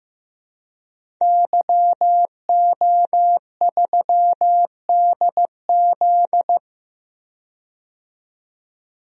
Morse code, heard as a keyed tone, about 15 wpm, 700 Hz. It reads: YO3DZ
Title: DXCC-100 Call Signs - 5 Characters - 15wpm